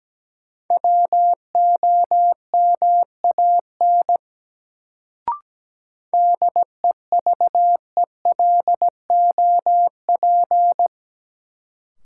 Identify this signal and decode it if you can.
Morse code, heard as a keyed tone, about 17 wpm, 700 Hz. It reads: WOMAN DEVELOP
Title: Sentences from Top 1000 Words - Review Part 2 - New Word Drill - 17wpm